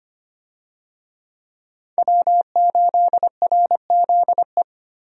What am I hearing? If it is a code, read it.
Morse code, heard as a keyed tone, about 25 wpm, 700 Hz. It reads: W8RZE
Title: USA Call Signs - 5 Characters - 25wpm